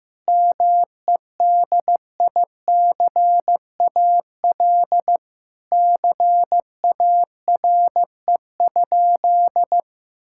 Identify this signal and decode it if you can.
Morse code, heard as a keyed tone, about 16 wpm, 700 Hz. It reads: MEDICAL CARE?
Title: Sentences from Top 1000 Words - Review Part 5 - 15wpm